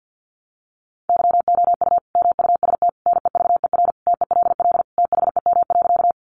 Morse code, heard as a keyed tone, about 50 wpm, 700 Hz. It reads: YOU MUST NEVER TELL THEM.
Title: Sentences from Top 900 Words 50wpm